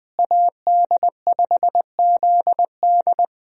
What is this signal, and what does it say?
Morse code, heard as a keyed tone, about 20 wpm, 700 Hz. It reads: AD5ZD
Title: USA Call Signs - 5 Characters - Repeated 3x - 20wpm